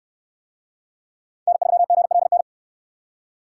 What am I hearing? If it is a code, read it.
Morse code, heard as a keyed tone, about 68 wpm, 700 Hz. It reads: N3GLN